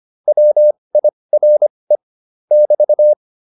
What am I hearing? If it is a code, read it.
Morse code, heard as a keyed tone, about 25 wpm, spaced as for 20 wpm, 600 Hz. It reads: WIRE <BT>